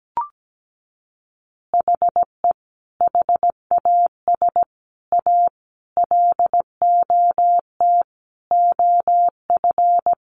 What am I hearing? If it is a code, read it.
Morse code, heard as a keyed tone, about 17 wpm, 700 Hz. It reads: HE HAS A LOT OF
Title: Additional Easy Phrases 17wpm